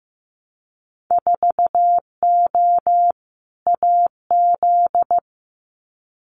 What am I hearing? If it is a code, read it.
Morse code, heard as a keyed tone, about 15 wpm, 700 Hz. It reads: 4O AZ
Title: Winter Field Day Exchanges Encore 15wpm